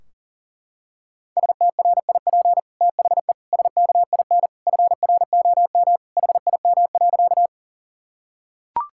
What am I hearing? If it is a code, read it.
Morse code, heard as a keyed tone, about 40 wpm, 700 Hz. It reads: STRIP THE SKIN FROM HIM.
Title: Sentences from Top 1600 Words 40wpm